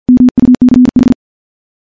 Morse code, heard as a keyed tone, about 52 wpm, 250 Hz. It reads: MUCH
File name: morse / morse-training-50WPM